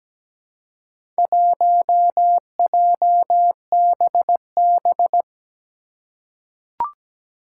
Morse code, heard as a keyed tone, about 17 wpm, 700 Hz. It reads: 1JBB
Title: Four Challenging Letter-Number Combinations 17wpm